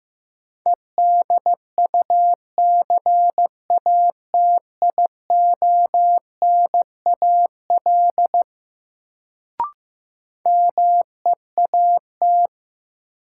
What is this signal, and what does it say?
Morse code, heard as a keyed tone, about 16 wpm, 700 Hz. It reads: EDUCATIONAL MEAT
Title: Sentences from Top 1200 Words - New Word Drill Alt - 15wpm